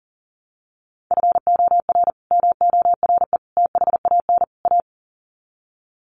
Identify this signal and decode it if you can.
Morse code, heard as a keyed tone, about 40 wpm, 700 Hz. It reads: FOR MORE THAN A
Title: Sets of 4 Words 40wpm